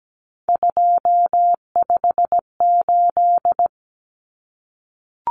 Morse code, heard as a keyed tone, about 17 wpm, 700 Hz. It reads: 258
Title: Numbers - 3-Digit 17wpm